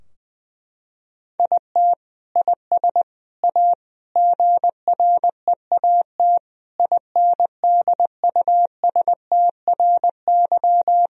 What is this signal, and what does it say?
Morse code, heard as a keyed tone, about 20 wpm, 700 Hz. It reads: IT IS A GREAT INDUSTRY
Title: Sentences from Top 1000 Words - Review Part 4 - 20wpm